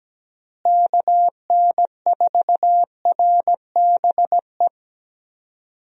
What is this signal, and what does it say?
Morse code, heard as a keyed tone, about 17 wpm, 700 Hz. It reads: KN4RBE
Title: USA Call Signs - 6 Characters - Repeated 3x - 17wpm